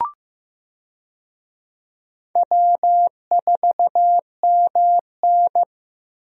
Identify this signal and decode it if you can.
Morse code, heard as a keyed tone, about 15 wpm, 700 Hz. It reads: W4MN
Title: USA Call Signs - 4 Characters - Repeated 3x - 15wpm